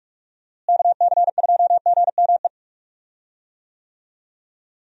Morse code, heard as a keyed tone, about 46 wpm, 700 Hz. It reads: KC2GME
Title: USA Call Signs - 6 Characters - Repeated 2x - 45wpm